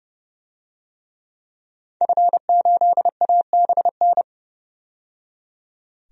Morse code, heard as a keyed tone, about 30 wpm, 700 Hz. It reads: F8ABN
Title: DXCC-100 Call Signs - 5 Characters - 30wpm